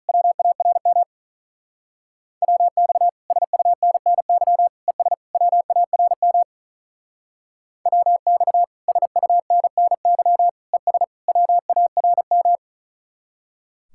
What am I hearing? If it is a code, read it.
Morse code, heard as a keyed tone, about 43 wpm, 700 Hz. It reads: WARM WX SUNNY ES WARM WX SUNNY ES WARM
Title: Speed-Racing Common QSO Phrases - 35wpm